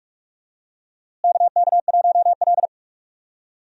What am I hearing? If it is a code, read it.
Morse code, heard as a keyed tone, about 41 wpm, 700 Hz. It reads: KK1L
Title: USA Call Signs - 4 Characters - Repeated 2x - 45wpm